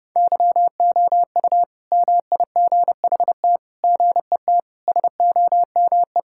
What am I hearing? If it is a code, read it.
Morse code, heard as a keyed tone, about 30 wpm, 700 Hz. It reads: YOU MIGHT GET SOME
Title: Sentences from Top 1000 Words - Review Part 5 - 30wpm